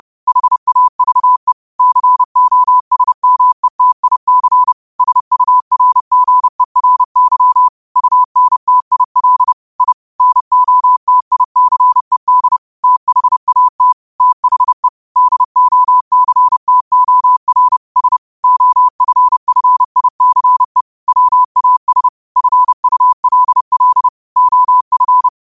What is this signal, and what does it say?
Morse code, heard as a keyed tone, about 30 wpm, 1000 Hz. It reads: HAVE COSMETIC SURGERY UNTIL I NOTICED THAT THE DOCTOR S OFFICE WAS FULL OF
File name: qotdfiles / 30wpm